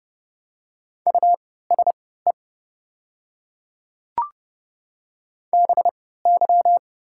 Morse code, heard as a keyed tone, about 30 wpm, spaced as for 17 wpm, 700 Hz. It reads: USE BY
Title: 3x Character Spacing - Top 500 Words - 30wpm